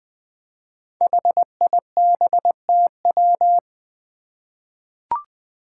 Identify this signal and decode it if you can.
Morse code, heard as a keyed tone, about 21 wpm, 700 Hz. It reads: HIBTW
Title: Five Challenging Letter-Number Combinations 20wpm